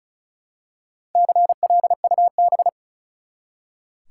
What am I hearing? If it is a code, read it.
Morse code, heard as a keyed tone, about 36 wpm, 700 Hz. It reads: CLUB
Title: Four Letter Words - 35wpm